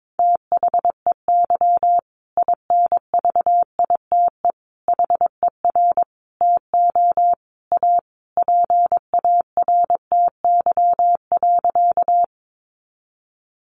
Morse code, heard as a keyed tone, about 22 wpm, 700 Hz. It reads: THEY INVITE HER TO A PARTY.
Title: Easy Phrases 22wpm